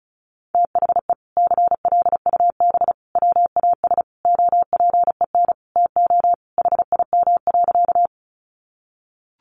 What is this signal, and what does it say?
Morse code, heard as a keyed tone, about 34 wpm, 700 Hz. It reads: THE CLUB WAS OPEN TO HIM.
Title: Sentences from Top 1200 Words 35wpm